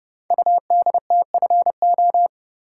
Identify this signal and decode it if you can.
Morse code, heard as a keyed tone, about 28 wpm, 700 Hz. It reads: UDTFO
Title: Code Groups - 5 Letters - 30wpm